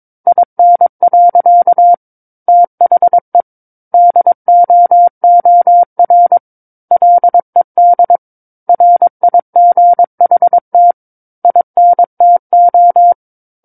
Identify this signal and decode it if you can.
Morse code, heard as a keyed tone, about 22 wpm, 700 Hz. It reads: IN. THE DOOR LED RIGHT INTO
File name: 06-Chapter-06 22 WPM